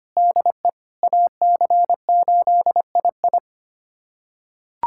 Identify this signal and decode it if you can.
Morse code, heard as a keyed tone, about 25 wpm, 700 Hz. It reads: DE AC8II
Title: USA Callsign DE Callsign - 5 Characters - 25wpm